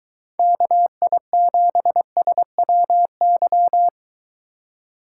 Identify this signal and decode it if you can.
Morse code, heard as a keyed tone, about 23 wpm, 700 Hz. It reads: KI7SWY